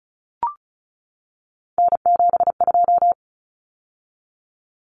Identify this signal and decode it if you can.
Morse code, heard as a keyed tone, about 34 wpm, 700 Hz. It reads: N72
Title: Common Cut Numbers - 3-Digit 35wpm